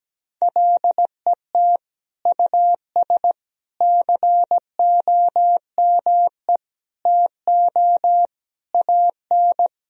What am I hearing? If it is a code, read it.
Morse code, heard as a keyed tone, about 17 wpm, 700 Hz. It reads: LET US COME TO AN
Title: Sentences from Top 1000 Words - Review Part 3 - 17wpm